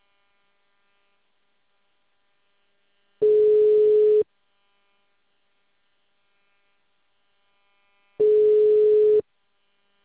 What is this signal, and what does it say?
Ringback tone (Europe, 425 Hz).
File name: out- -2044- - - .33877